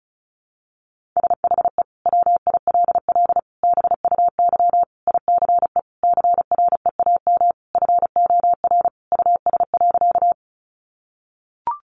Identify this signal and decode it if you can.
Morse code, heard as a keyed tone, about 34 wpm, 700 Hz. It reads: SHE WILL BUY ICE CREAM FOR US.
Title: Sentences from Top 1500 Words 35wpm